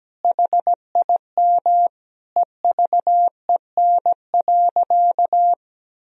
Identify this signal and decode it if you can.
Morse code, heard as a keyed tone, about 17 wpm, 700 Hz. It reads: HIM EVEN.
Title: Sentences from Top 1000 Words - Review Part 1 - 17wpm